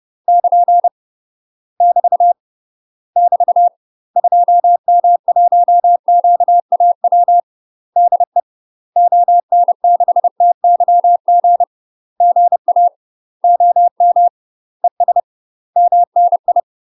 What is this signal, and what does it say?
Morse code, heard as a keyed tone, about 30 wpm, 700 Hz. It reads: <KN> <BT> <BT> 2M1QAW DE ON6TYG GA OM ES MNI
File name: UK MASTER QSOs TRAINING / 30WPM / UK MASTER QSOs 0033